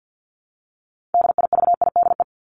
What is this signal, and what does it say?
Morse code, heard as a keyed tone, about 50 wpm, 700 Hz. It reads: DIVIDE
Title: Top 1501-1750 Words 50wpm